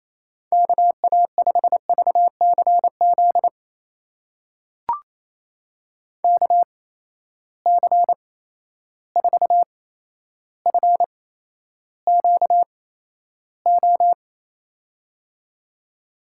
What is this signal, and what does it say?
Morse code, heard as a keyed tone, about 28 wpm, spaced as for 12 wpm, 700 Hz. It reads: KA5VCZ KC4FQO
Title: USA Call Signs - 6 Characters - 8x Character Spacing - 28wpm